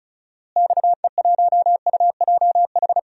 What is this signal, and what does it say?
Morse code, heard as a keyed tone, about 34 wpm, 700 Hz. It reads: XE1UJH
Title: DXCC-100 Call Signs - 6 Characters - Repeated 3x - 35wpm